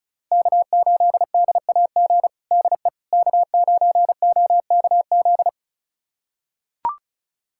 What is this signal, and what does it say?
Morse code, heard as a keyed tone, about 36 wpm, 700 Hz. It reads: K8DAG DE K9OKZ
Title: USA Callsign DE Callsign - 5 Characters - 35wpm